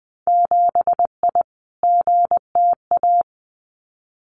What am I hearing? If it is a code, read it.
Morse code, heard as a keyed tone, about 20 wpm, 700 Hz. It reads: 7I GTA
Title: Winter Field Day Exchanges Encore 20wpm